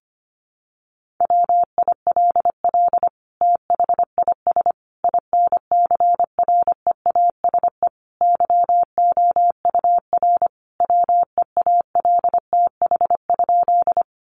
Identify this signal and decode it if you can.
Morse code, heard as a keyed tone, about 25 wpm, 700 Hz. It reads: WILL THIS INCREASE YOUR WEALTH?
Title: Sentences from Top 1200 Words 25wpm